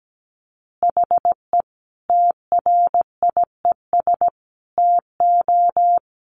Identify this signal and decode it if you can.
Morse code, heard as a keyed tone, about 17 wpm, 700 Hz. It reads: HE TRIES TO
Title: Additional Easy Phrases 17wpm